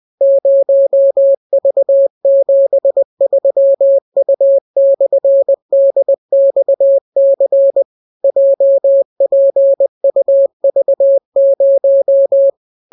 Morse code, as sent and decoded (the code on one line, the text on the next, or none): ----- ...- --... ...-- ..- -..-. -.. -..- -.-. / .--- .--. ..- ...- -----
0V73U/DXC JPUV0